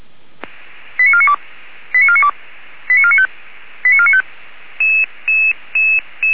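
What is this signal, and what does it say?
Five-tone selective call (ZVEI), heard as quick runs of five tones, tones of about 70 ms. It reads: 89492 89492 89495 89495
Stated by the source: ZVEI five-tone selective call received on an SDR.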